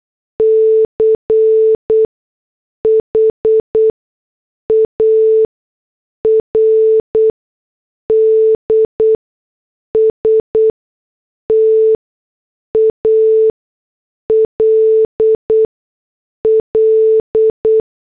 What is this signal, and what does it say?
Morse code, heard as a keyed tone, about 8 wpm, spaced as for 6 wpm, 440 Hz. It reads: CHARD STALL
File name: morse / headlines